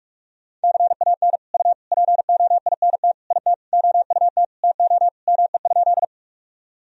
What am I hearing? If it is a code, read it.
Morse code, heard as a keyed tone, about 42 wpm, 700 Hz. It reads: CAN U POINT IT OUT TO ME?
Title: Sentences from Top 200 Words and Abbreviations 45wpm